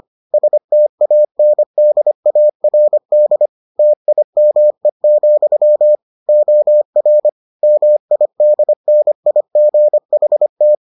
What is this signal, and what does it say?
Morse code, heard as a keyed tone, about 25 wpm, 600 Hz. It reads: STANDARD TIME, OR MIDNIGHT